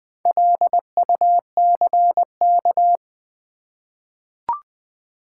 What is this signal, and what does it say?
Morse code, heard as a keyed tone, about 20 wpm, 700 Hz. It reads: LUCK
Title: Four Letter Words - 20wpm